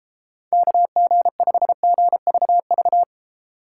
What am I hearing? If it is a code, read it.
Morse code, heard as a keyed tone, about 34 wpm, 700 Hz. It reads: KG5GVV
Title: Speed Racing - USA 2x3 Call Signs 28wpm